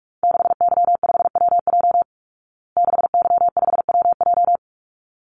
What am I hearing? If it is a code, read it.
Morse code, heard as a keyed tone, about 44 wpm, 700 Hz. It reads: 6Y5WJ 6Y5WJ
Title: Call Signs Repeated 3x - 45wpm